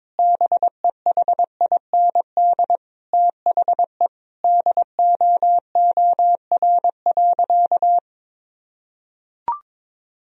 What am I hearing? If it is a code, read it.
Morse code, heard as a keyed tone, about 22 wpm, 700 Hz. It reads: BEHIND THE DOOR.
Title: Sentences from Top 1000 Words - Review Part 2 - 22wpm